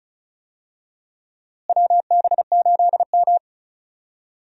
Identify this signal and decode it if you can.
Morse code, heard as a keyed tone, about 34 wpm, 700 Hz. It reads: WB8M
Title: USA Call Signs - 4 Characters - Repeated 3x - 35wpm